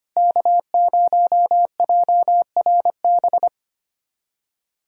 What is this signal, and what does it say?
Morse code, heard as a keyed tone, about 25 wpm, 700 Hz. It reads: K0JRB
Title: USA Callsign DE Callsign - 5 Characters - 25wpm